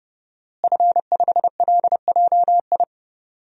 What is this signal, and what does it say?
Morse code, heard as a keyed tone, about 30 wpm, 700 Hz. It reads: F5LJI
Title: DXCC-100 Call Signs - 5 Characters - Repeated 3x - 30wpm